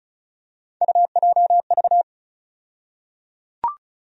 Morse code, heard as a keyed tone, about 34 wpm, 700 Hz. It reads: UJV
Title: USA Call Sign Postfixes - 3 Letters 35wpm